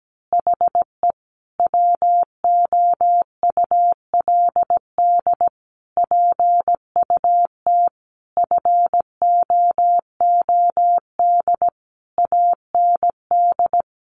Morse code, heard as a keyed tone, about 17 wpm, 700 Hz. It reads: HE WOULD PUT FOOD AND
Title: Easy Phrases 17wpm